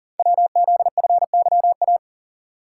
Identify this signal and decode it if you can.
Morse code, heard as a keyed tone, about 40 wpm, 700 Hz. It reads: WZFYA
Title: Code Groups - 5 Letters - 40wpm